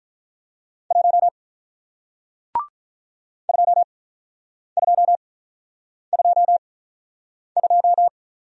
Morse code, heard as a keyed tone, about 47 wpm, spaced as for 9 wpm, 700 Hz. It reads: 1 2222